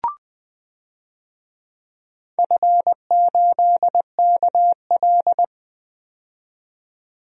Morse code, heard as a keyed tone, about 20 wpm, 700 Hz. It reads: F8KL
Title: DXCC-100 Call Signs - 4 Characters - 20wpm